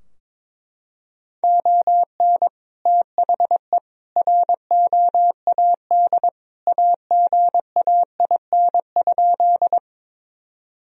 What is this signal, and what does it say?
Morse code, heard as a keyed tone, about 21 wpm, 700 Hz. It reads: ON THE ROAD AGAIN?
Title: Sentences from Top 1000 Words - Review Part 2 - 22wpm